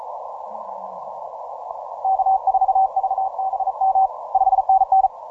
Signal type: Morse code